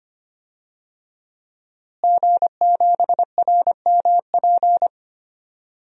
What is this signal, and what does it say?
Morse code, heard as a keyed tone, about 25 wpm, 700 Hz. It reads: G7RMP
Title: DXCC-100 Call Signs - 5 Characters - Repeated 3x - 25wpm